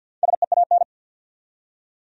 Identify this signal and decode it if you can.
Morse code, heard as a keyed tone, about 47 wpm, 700 Hz. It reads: SEAN
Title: USA Common Names - 50wpm